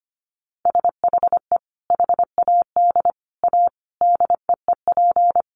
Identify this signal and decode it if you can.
Morse code, heard as a keyed tone, about 25 wpm, 700 Hz. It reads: SHE HAD A DEEP
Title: Sentences from Top 1300 Words 25wpm